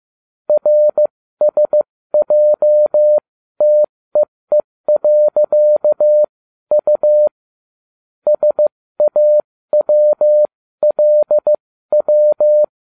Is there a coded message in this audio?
Yes: Morse code from 0.5 to 12.6 s, heard as a keyed tone, about 15 wpm, spaced as for 13 wpm, 600 Hz. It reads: RSJTEE.U SAWLW